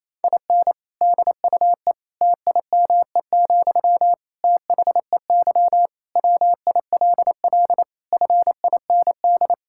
Morse code, heard as a keyed tone, about 27 wpm, 700 Hz. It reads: IN DUE TIME, THEY WILL FIND
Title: Sentences from Top 1000 Words - Review Part 6 - 28wpm